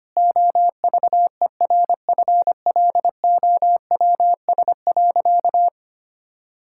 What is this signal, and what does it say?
Morse code, heard as a keyed tone, about 25 wpm, 700 Hz. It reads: OVERFLOWS.
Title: Pangram Challenge v2 - 25wpm